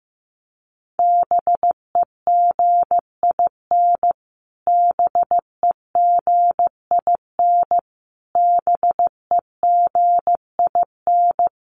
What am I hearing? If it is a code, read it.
Morse code, heard as a keyed tone, about 15 wpm, 700 Hz. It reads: BEGIN BEGIN BEGIN
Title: Top 300 Words Repeated 15wpm